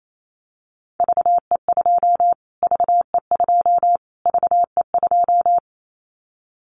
Morse code, heard as a keyed tone, about 28 wpm, 700 Hz. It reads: VE2 VE2 VE2
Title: Callsign Prefixes for Top 25 DXCC - Repeated 3x - 28wpm